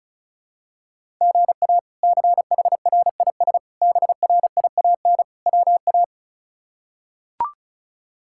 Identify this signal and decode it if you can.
Morse code, heard as a keyed tone, about 36 wpm, 700 Hz. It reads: GA CHRIS BRIAN WA
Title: K1USN SST QSO Practice - 35wpm